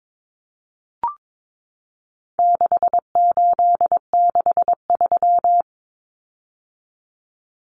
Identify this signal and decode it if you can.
Morse code, heard as a keyed tone, about 22 wpm, 700 Hz. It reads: 6863
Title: Common Cut Numbers - 4-Digit 22wpm